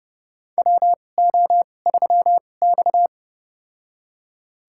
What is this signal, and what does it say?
Morse code, heard as a keyed tone, about 30 wpm, spaced as for 22 wpm, 700 Hz. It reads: WO3X
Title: USA Call Signs - 4 Characters - 2x Character Spacing - 30wpm